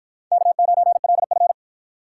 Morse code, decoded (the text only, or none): K9LF